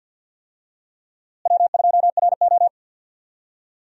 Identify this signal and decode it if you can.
Morse code, heard as a keyed tone, about 52 wpm, 700 Hz. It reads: W2RO
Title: USA Call Signs - 4 Characters - Repeated 3x - 50wpm